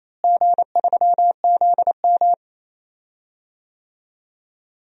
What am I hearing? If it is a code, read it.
Morse code, heard as a keyed tone, about 27 wpm, 700 Hz. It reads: G3ZM
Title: DXCC-100 Call Signs - 4 Characters - Repeated 3x - 28wpm